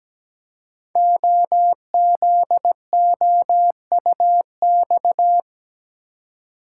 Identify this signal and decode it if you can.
Morse code, heard as a keyed tone, about 17 wpm, 700 Hz. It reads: OZOUX
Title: Code Groups - 5 Letter:Numbers - 17wpm